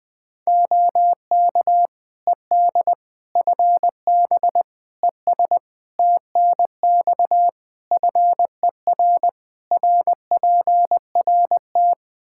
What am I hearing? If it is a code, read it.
Morse code, heard as a keyed tone, about 20 wpm, 700 Hz. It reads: OK ED FB ES TNX FER RPRT
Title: K1USN SST QSO Practice 20wpm